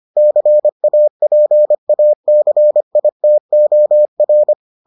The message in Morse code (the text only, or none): CAPACITOR